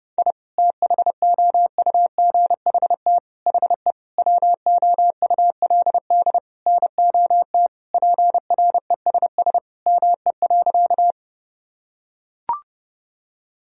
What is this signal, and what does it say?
Morse code, heard as a keyed tone, about 30 wpm, 700 Hz. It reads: I THOUGHT HE WOULD NOT PRESS ME.
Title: Sentences from Top 1000 Words - Review Part 1 - 30wpm